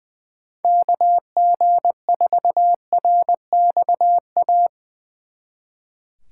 Morse code, heard as a keyed tone, about 23 wpm, 700 Hz. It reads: KG4RXA
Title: Speed Racing - USA Call Signs 20wpm